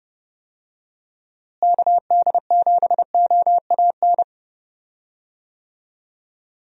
Morse code, heard as a keyed tone, about 30 wpm, 700 Hz. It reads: KD7OAN
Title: USA Call Signs - 6 Characters - Repeated 3x - 30wpm